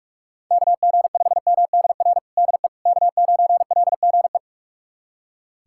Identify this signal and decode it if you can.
Morse code, heard as a keyed tone, about 41 wpm, 700 Hz. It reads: KG5MDR DE K9LGE
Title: USA Callsign DE Callsign 45wpm